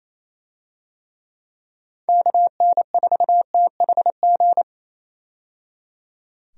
Morse code, heard as a keyed tone, about 28 wpm, 700 Hz. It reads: KN4THG
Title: USA Call Signs - 6 Characters - 28wpm